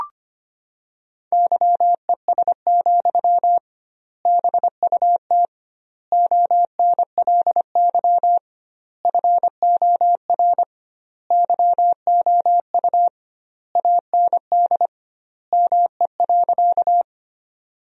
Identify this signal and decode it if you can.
Morse code, heard as a keyed tone, about 25 wpm, 700 Hz. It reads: YES, BUT ONLY FOR YOU AND ME.
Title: 2x Word-Spacing - Sentences from Top 500 Words - 25wpm